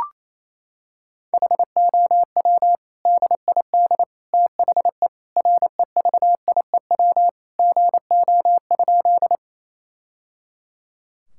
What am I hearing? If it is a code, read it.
Morse code, heard as a keyed tone, about 28 wpm, 700 Hz. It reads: HOW DID THE REVIEW GO?
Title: Sentences from Top 1000 Words - Review Part 6 - 28wpm